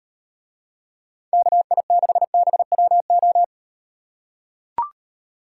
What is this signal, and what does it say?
Morse code, heard as a keyed tone, about 36 wpm, 700 Hz. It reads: KI6BWO